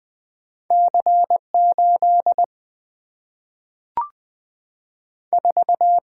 Morse code, heard as a keyed tone, about 20 wpm, 700 Hz. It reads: C8 4
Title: Single-Letter Single-Number 20wpm